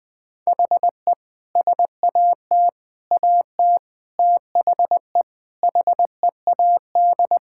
Morse code, heard as a keyed tone, about 20 wpm, 700 Hz. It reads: HE SAT AT THE HEAD
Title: Sentences with 5-Word Phrases - Spoken Beginning - 20wpm